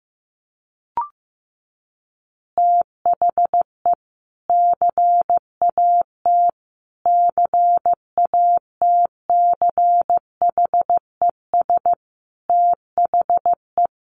Morse code, heard as a keyed tone, about 14 wpm, 700 Hz. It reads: THE CAT CATCHES THE
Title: Additional Easy Phrases 15wpm